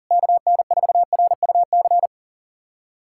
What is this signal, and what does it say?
Morse code, heard as a keyed tone, about 37 wpm, 690 Hz. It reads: KN4RUC
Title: USA Callsign DE Callsign - 6 Characters - 40wpm